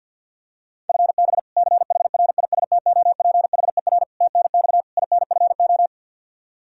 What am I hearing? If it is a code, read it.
Morse code, heard as a keyed tone, about 53 wpm, 700 Hz. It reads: FB CHRISTOPHER TNX INFO
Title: K1USN SST QSO Practice 50wpm